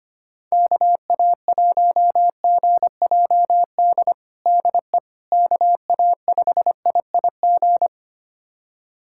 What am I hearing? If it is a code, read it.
Morse code, heard as a keyed tone, about 25 wpm, 700 Hz. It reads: KA1GJD DE KA5IIG
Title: USA Callsign DE Callsign - 6 Characters - 25wpm